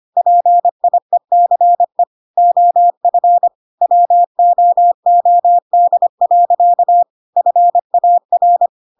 Morse code, heard as a keyed tone, about 25 wpm, 700 Hz. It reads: PIECE OF WOOD. FAR
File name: Pinocchio / Pinocchio ch1 25 25 ik1hge